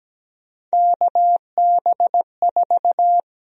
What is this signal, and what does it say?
Morse code, heard as a keyed tone, about 17 wpm, 700 Hz. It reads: KB4
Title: USA Call Sign Prefixes - 17wpm